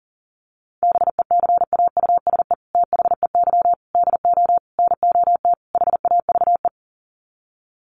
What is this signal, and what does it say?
Morse code, heard as a keyed tone, about 40 wpm, 700 Hz. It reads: BECAUSE THEY DO NOT HAVE
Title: Sets of 5 Words Encore2 40wpm